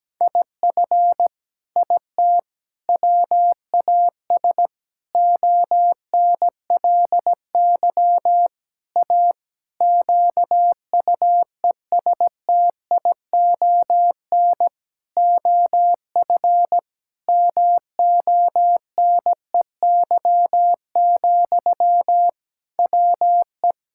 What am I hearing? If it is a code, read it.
Morse code, heard as a keyed tone, about 17 wpm, 700 Hz. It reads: IF IT WAS ONLY A QUESTION OF MONEY, WE
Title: Sentences from Top 1600 Words 17wpm